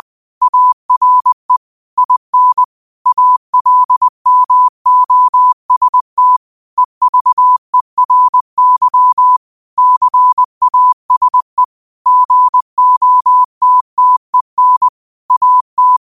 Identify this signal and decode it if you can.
Morse code, heard as a keyed tone, about 20 wpm, 1000 Hz. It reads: ARE IN ALMOST EVERY CASE GOTTEN AT
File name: qotdfiles / 20wpm